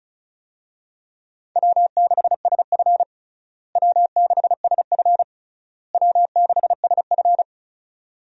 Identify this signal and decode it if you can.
Morse code, heard as a keyed tone, about 36 wpm, 700 Hz. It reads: W6SF W6SF W6SF
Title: USA Call Signs - 4 Characters - Repeated 3x - 35wpm